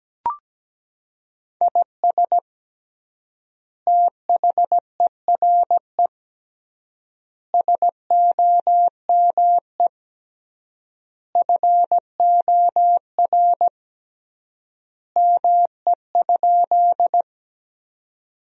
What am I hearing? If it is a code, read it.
Morse code, heard as a keyed tone, about 17 wpm, 700 Hz. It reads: IS THERE SOME FOR ME?
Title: 3x Word-Spacing - Sentences from Top 200 Words - 17wpm